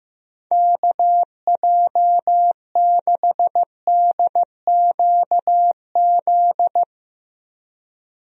Morse code, heard as a keyed tone, about 15 wpm, 700 Hz. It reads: KJ6DQZ
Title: USA Call Signs - 6 Characters - Repeated 2x - 15wpm